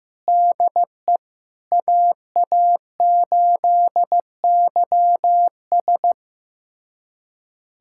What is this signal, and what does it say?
Morse code, heard as a keyed tone, about 15 wpm, 700 Hz. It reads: DE AA8YS
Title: USA Callsign DE Callsign 15wpm